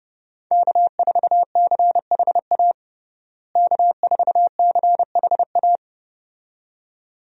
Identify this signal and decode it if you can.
Morse code, heard as a keyed tone, about 31 wpm, 700 Hz. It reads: K4CHA K4CHA
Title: USA Call Signs - 5 Characters - Repeated 2x - 30wpm